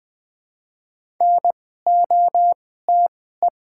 Morse code, heard as a keyed tone, about 20 wpm, spaced as for 14 wpm, 690 Hz. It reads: NOTE
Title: 2x Character Spacing - Top 500 Words - 20wpm